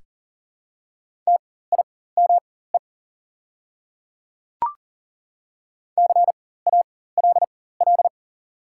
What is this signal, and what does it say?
Morse code, heard as a keyed tone, about 41 wpm, spaced as for 19 wpm, 700 Hz. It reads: TIME CALL
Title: 4x Character Spacing - Top 200 Words - 40wpm